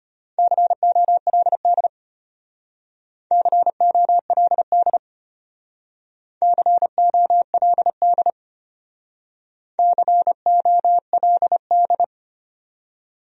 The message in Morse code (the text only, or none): COLD COLD COLD COLD